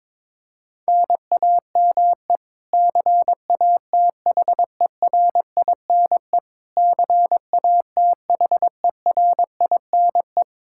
Morse code, heard as a keyed tone, about 22 wpm, 700 Hz. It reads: NAME CATHERINE CATHERINE
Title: Practice QSO - Exchange 1 - Beginning - 22wpm